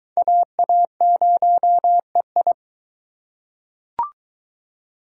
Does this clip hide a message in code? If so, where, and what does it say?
Yes: Morse code from 0.2 to 2.5 s, heard as a keyed tone, about 23 wpm, 700 Hz. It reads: AA0EI